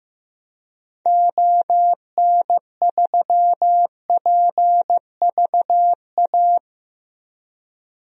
Morse code, heard as a keyed tone, about 15 wpm, 700 Hz. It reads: ON3PVA
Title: DXCC-100 Call Signs - 6 Characters - Repeated 2x - 15wpm